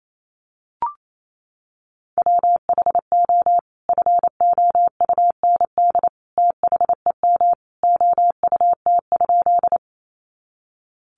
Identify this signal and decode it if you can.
Morse code, heard as a keyed tone, about 28 wpm, 700 Hz. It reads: WHO FOUND THEM OUT?
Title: Sentences from Top 300 Words 28wpm